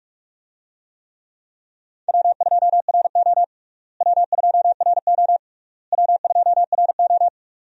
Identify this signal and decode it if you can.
Morse code, heard as a keyed tone, about 41 wpm, 700 Hz. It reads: W2RO W2RO W2RO
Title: USA Call Signs - 4 Characters - Repeated 3x - 45wpm